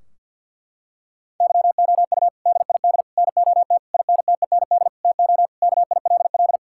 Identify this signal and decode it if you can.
Morse code, heard as a keyed tone, about 50 wpm, 700 Hz. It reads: YOU DID NOT INTEND TO KILL
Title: Sentences from Top 1600 Words 50wpm